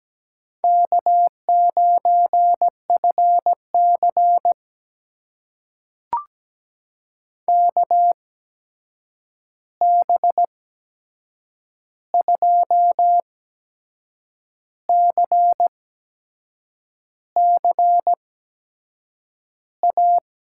Morse code, heard as a keyed tone, about 17 wpm, spaced as for 6 wpm, 700 Hz. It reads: K9FC KB2CCA
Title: USA Call Signs - 8x Character Spacing - 17wpm